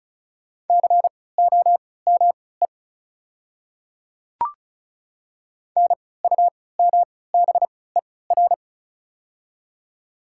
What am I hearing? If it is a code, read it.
Morse code, heard as a keyed tone, about 36 wpm, spaced as for 20 wpm, 700 Hz. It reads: COME NUMBER
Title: 3x Character Spacing - Top 100 Words - 35wpm